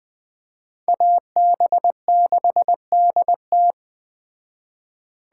Morse code, heard as a keyed tone, about 20 wpm, 700 Hz. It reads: AB6DT